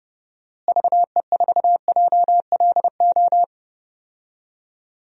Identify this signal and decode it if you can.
Morse code, heard as a keyed tone, about 29 wpm, 700 Hz. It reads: VE4JLO
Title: DXCC-100 Call Signs - 6 Characters - Repeated 2x - 30wpm